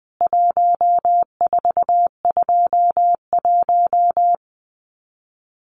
Morse code, heard as a keyed tone, about 20 wpm, 700 Hz. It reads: 1421
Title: CWT Exchange 20wpm